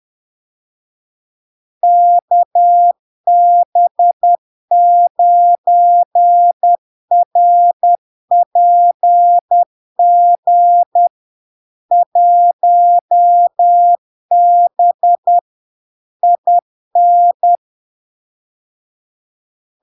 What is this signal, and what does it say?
Morse code, heard as a keyed tone, about 10 wpm, 700 Hz. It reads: KB9RPG 1B IN
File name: LARK-FD-2022-Log-in-CW-10wpm-700-Hz